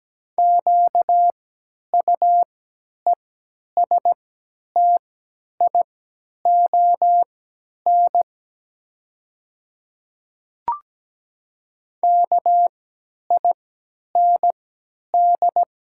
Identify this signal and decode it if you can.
Morse code, heard as a keyed tone, about 17 wpm, spaced as for 10 wpm, 690 Hz. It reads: QUESTION KIND
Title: 3x Character Spacing - Top 500 Words - 17wpm